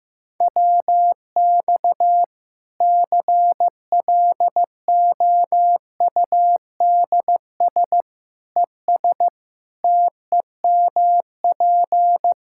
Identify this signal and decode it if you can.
Morse code, heard as a keyed tone, about 16 wpm, 700 Hz. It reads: WX CLOUDS ES TEMP
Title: Practice QSO - Exchange 2 - Info - 15wpm